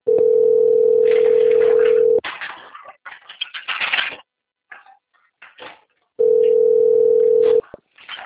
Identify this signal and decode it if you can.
Ringback tone (North America), 440 + 480 Hz.
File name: out- - - - - .33235